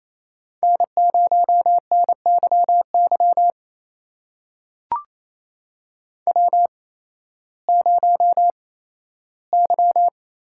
Morse code, heard as a keyed tone, about 28 wpm, spaced as for 15 wpm, 700 Hz. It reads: N0NYY W0Y
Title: USA Call Signs - 8x Character Spacing - 28wpm